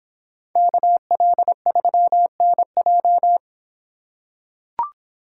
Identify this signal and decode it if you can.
Morse code, heard as a keyed tone, about 26 wpm, 700 Hz. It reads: KL3NJ